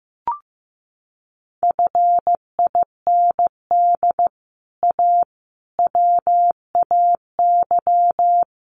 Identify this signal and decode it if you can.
Morse code, heard as a keyed tone, about 15 wpm, 700 Hz. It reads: FIND A WAY
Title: Sets of 4 Words 15wpm